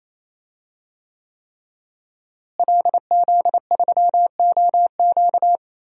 Morse code, heard as a keyed tone, about 28 wpm, 700 Hz. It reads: LZ3OQ
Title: DXCC-100 Call Signs - 5 Characters - Repeated 3x - 28wpm